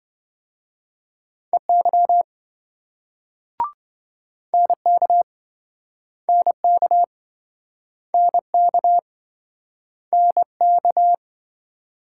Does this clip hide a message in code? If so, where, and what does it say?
Yes: Morse code from 1.5 to 11.2 s, heard as a keyed tone, about 26 wpm, 700 Hz. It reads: EY NK NK NK NK